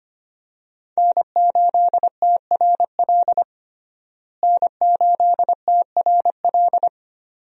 Morse code, heard as a keyed tone, about 25 wpm, 700 Hz. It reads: N8TRL N8TRL
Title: USA Call Signs - 5 Characters - Repeated 2x - 25wpm